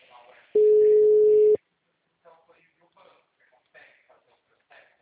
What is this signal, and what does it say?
Ringback tone (Europe, 425 Hz), with speech.